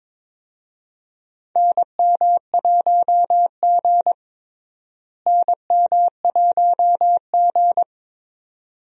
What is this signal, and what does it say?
Morse code, heard as a keyed tone, about 21 wpm, 700 Hz. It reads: NM1G NM1G
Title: USA Call Signs - 4 Characters - Repeated 2x - 22wpm